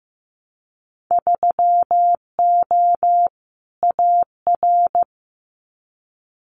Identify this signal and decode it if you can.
Morse code, heard as a keyed tone, about 15 wpm, 700 Hz. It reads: 3O AR
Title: Winter Field Day Exchanges Encore 15wpm